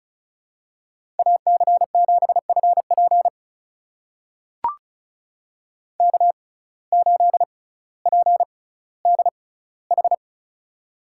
Morse code, heard as a keyed tone, about 34 wpm, spaced as for 18 wpm, 700 Hz. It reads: AC7FP K8PDH
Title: USA Call Signs - 5 Characters - 6x Character Spacing - 35wpm